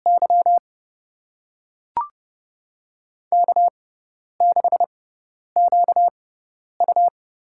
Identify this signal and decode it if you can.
Morse code, heard as a keyed tone, about 30 wpm, spaced as for 10 wpm, 700 Hz. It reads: Y K6QU
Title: USA Call Signs - 4 Characters - 6x Character Spacing - 30wpm